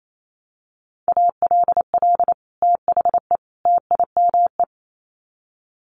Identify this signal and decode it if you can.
Morse code, heard as a keyed tone, about 28 wpm, 700 Hz. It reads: ALL THE TIME
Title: Sets of 3 Words 28wpm